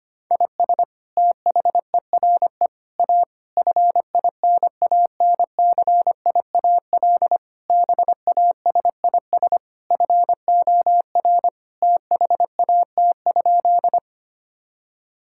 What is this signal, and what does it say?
Morse code, heard as a keyed tone, about 25 wpm, 700 Hz. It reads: IS THERE A FINANCIAL BASIS FOR THAT?
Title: Sentences from Top 1000 Words - Review Part 5 - 25wpm